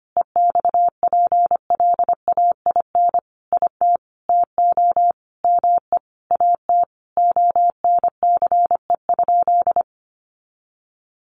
Morse code, heard as a keyed tone, about 25 wpm, 700 Hz. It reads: EXPLAIN IT TO ME AT ONCE?
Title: Sentences from Top 1100 Words 25wpm